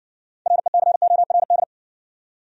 Morse code, heard as a keyed tone, about 60 wpm, 700 Hz. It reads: RECORD